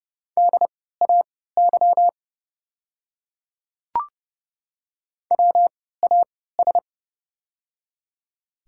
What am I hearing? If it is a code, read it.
Morse code, heard as a keyed tone, about 30 wpm, spaced as for 17 wpm, 700 Hz. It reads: DAY WAS
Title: 3x Character Spacing - Top 100 Words - 30wpm